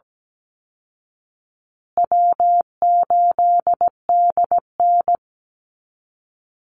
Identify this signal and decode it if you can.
Morse code, heard as a keyed tone, about 17 wpm, 700 Hz. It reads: W8DN
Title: Easy Call Signs Repeated 2x - 17wpm